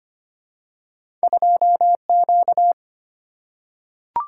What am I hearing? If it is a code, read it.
Morse code, heard as a keyed tone, about 25 wpm, 690 Hz. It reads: 2Q
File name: Two Letter-Number Combinations - 25wpm